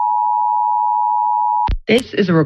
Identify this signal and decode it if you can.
Emergency Alert System (SAME) tones (the 853 and 960 Hz attention tone), with speech. It is heard from 0.0 to 1.7 s.